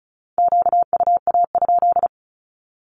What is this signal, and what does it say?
Morse code, heard as a keyed tone, about 36 wpm, 700 Hz. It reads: QUA?
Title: Q-Codes with Spoken Meaning - 35wpm